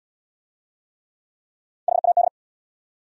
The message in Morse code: HAD

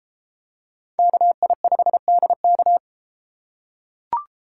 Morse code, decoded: KI5DK

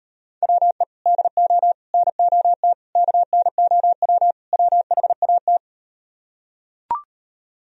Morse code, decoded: WE DO NOT KNOW WHAT

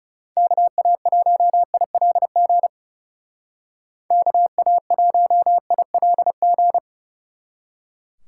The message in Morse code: KA1ILG KA1ILG

